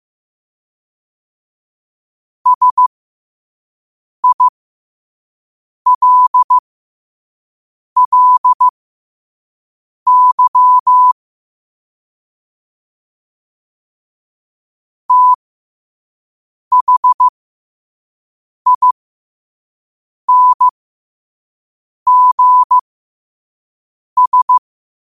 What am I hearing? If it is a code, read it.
Morse code, heard as a keyed tone, about 15 wpm, spaced as for 5 wpm, 1000 Hz. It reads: SILLY THINGS
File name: qotdfiles / 5wpm